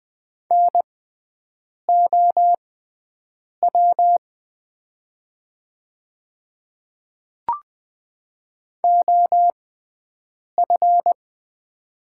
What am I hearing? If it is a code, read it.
Morse code, heard as a keyed tone, about 20 wpm, spaced as for 7 wpm, 700 Hz. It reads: NOW OF
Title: 6x Character Spacing - Top 300 Words - 20wpm